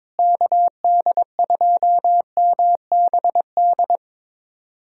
Morse code, heard as a keyed tone, about 22 wpm, 700 Hz. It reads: KD2MBD
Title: USA Callsign DE Callsign - 6 Characters - 22wpm